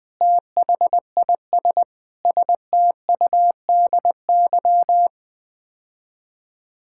Morse code, heard as a keyed tone, about 20 wpm, 700 Hz. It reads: THIS STUDY
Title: Build A Phrase - 5-Word Phrases - 20wpm